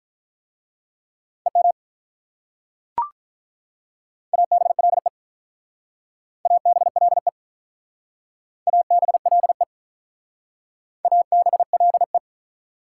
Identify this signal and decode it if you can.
Morse code, heard as a keyed tone, about 45 wpm, 700 Hz. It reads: EM ABLE ABLE ABLE ABLE